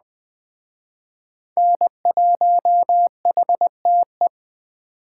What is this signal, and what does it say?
Morse code, heard as a keyed tone, about 21 wpm, 700 Hz. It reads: N1HTE